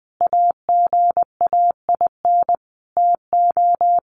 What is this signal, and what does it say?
Morse code, heard as a keyed tone, about 20 wpm, 700 Hz. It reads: AGAIN TO
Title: Sentences from Top 1500 Words 20wpm